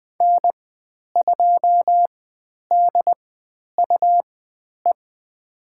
Morse code, heard as a keyed tone, about 20 wpm, spaced as for 10 wpm, 700 Hz. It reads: N2DUE
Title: 109 Lesson 18g - Letter m - Call Signs 20wpm